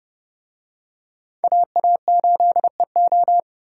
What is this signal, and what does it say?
Morse code, heard as a keyed tone, about 28 wpm, 700 Hz. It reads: AA8EO